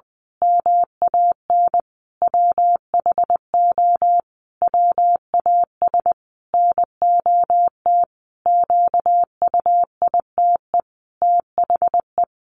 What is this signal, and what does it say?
Morse code, heard as a keyed tone, about 20 wpm, 700 Hz. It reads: MAN WHO WAS NOT QUITE THE
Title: A Sound Of Thunder 20wpm